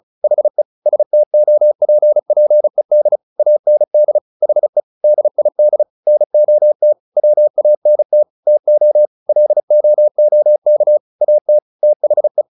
Morse code, heard as a keyed tone, about 34 wpm, 600 Hz. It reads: HE STOPPED AND HE DID NOT WANT TO LOOK AT THE
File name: the old man and the sea 2 35wpm